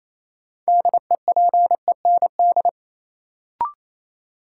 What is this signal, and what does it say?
Morse code, heard as a keyed tone, about 28 wpm, 700 Hz. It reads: DEPEND
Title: Sentences from Top 1300 Words - New Word Drill Alt - 28wpm